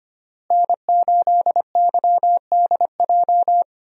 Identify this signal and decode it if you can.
Morse code, heard as a keyed tone, about 25 wpm, 700 Hz. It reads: N8YDJ
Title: USA Call Signs - 5 Characters - Repeated 3x - 25wpm